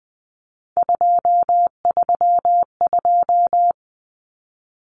Morse code, heard as a keyed tone, about 20 wpm, 700 Hz. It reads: 232
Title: Common Cut Numbers - 3-Digit 20wpm